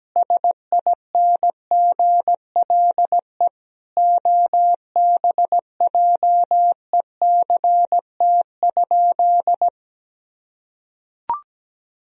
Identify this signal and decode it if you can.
Morse code, heard as a keyed tone, about 17 wpm, 700 Hz. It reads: SINGLE OBJECT?
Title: Sentences from Top 1000 Words - Review Part 4 - 17wpm